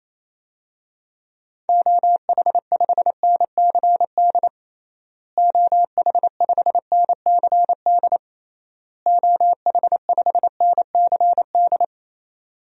Morse code, heard as a keyed tone, about 27 wpm, 700 Hz. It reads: OH5NCD OH5NCD OH5NCD
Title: DXCC-100 Call Signs - 6 Characters - Repeated 3x - 28wpm